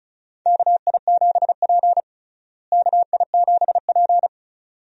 Morse code, heard as a keyed tone, about 36 wpm, 700 Hz. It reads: KI7P KI7P
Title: USA Call Signs - 4 Characters - Repeated 2x - 35wpm